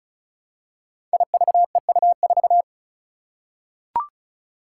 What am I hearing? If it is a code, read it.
Morse code, heard as a keyed tone, about 36 wpm, 700 Hz. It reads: IVEU4
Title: Five Challenging Letter-Number Combinations 35wpm